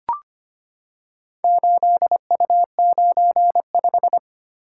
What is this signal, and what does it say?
Morse code, heard as a keyed tone, about 26 wpm, 690 Hz. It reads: 8U95
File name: practice / 25wpm / Four Letter-Number Combinations - 25wpm